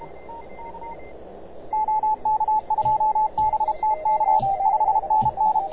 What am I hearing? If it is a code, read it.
Morse code, heard as a keyed tone, about 32 wpm, 860 Hz. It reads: OK2BTK 5NN